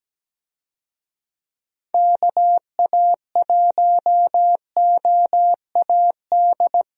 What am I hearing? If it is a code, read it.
Morse code, heard as a keyed tone, about 17 wpm, 700 Hz. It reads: KA1OAD
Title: USA Call Signs - 6 Characters - Repeated 3x - 17wpm